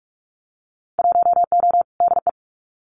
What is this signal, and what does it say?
Morse code, heard as a keyed tone, about 44 wpm, 700 Hz. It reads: 1O DE
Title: Winter Field Day Exchanges 45wpm